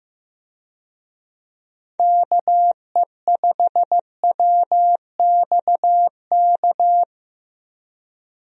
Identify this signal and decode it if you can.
Morse code, heard as a keyed tone, about 15 wpm, 700 Hz. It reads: KE5WXK
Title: USA Call Signs - 6 Characters - 15wpm